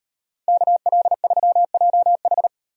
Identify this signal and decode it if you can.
Morse code, heard as a keyed tone, about 38 wpm, 700 Hz. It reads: KL3JH